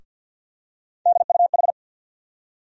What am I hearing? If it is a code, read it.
Morse code, heard as a keyed tone, about 52 wpm, 700 Hz. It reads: DUH
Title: USA Call Sign Postfixes - 3 Letters 50wpm